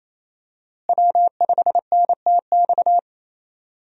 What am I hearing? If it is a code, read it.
Morse code, heard as a keyed tone, about 28 wpm, 700 Hz. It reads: W5NTX
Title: USA Call Signs - 5 Characters - Repeated 2x - 28wpm